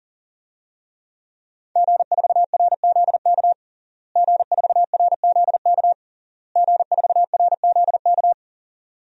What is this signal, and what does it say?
Morse code, heard as a keyed tone, about 40 wpm, 700 Hz. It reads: G4RZK G4RZK G4RZK
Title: DXCC-100 Call Signs - 5 Characters - Repeated 3x - 40wpm